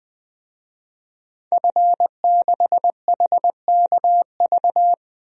Morse code, heard as a keyed tone, about 21 wpm, 700 Hz. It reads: F6HKV
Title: DXCC-100 Call Signs - 5 Characters - 20wpm